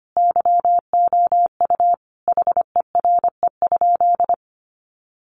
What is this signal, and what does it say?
Morse code, heard as a keyed tone, about 25 wpm, 700 Hz. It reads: YOU HERE?
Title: Sentences from Top 200 Words 25wpm